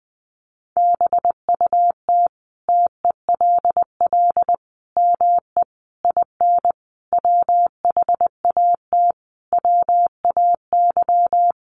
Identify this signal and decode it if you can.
Morse code, heard as a keyed tone, about 20 wpm, 700 Hz. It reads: BUT TELL ME IN WHAT WAY
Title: Sentences from Top 1300 Words 20wpm